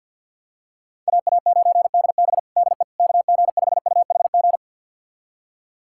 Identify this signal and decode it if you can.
Morse code, heard as a keyed tone, about 46 wpm, 700 Hz. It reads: AA9DB DE KG5UHG
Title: USA Callsign DE Callsign 50wpm